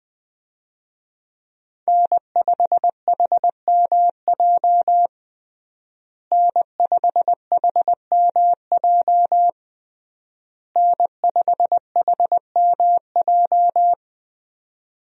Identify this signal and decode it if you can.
Morse code, heard as a keyed tone, about 20 wpm, 700 Hz. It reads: N5HMJ N5HMJ N5HMJ
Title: USA Call Signs - 5 Characters - Repeated 3x - 20wpm